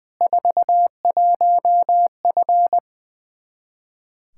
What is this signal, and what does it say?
Morse code, heard as a keyed tone, about 20 wpm, 700 Hz. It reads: 41F
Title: K1USN SST QSO Practice 20wpm